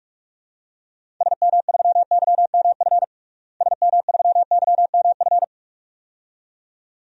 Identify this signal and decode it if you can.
Morse code, heard as a keyed tone, about 46 wpm, 700 Hz. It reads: SM3YMF SM3YMF
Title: DXCC-100 Call Signs - 6 Characters - Repeated 2x - 45wpm